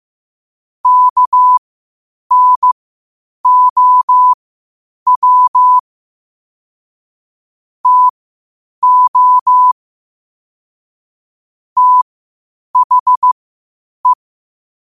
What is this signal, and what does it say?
Morse code, heard as a keyed tone, about 15 wpm, spaced as for 8 wpm, 1000 Hz. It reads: KNOW TO THE
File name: qotdfiles / 8wpm